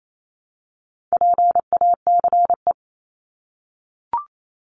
Morse code, heard as a keyed tone, about 28 wpm, 700 Hz. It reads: PACE
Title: Top 1751-2000 Words 28wpm